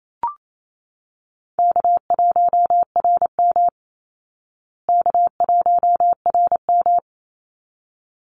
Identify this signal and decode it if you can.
Morse code, heard as a keyed tone, about 28 wpm, 700 Hz. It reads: K1RM K1RM
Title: Easy Call Signs Repeated 2x - 28wpm